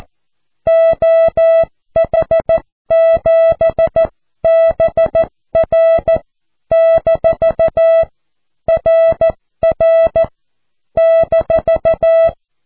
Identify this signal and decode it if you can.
Morse code, heard as a keyed tone, about 14 wpm, 660 Hz. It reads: OH7BR - RR -
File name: cw / QSO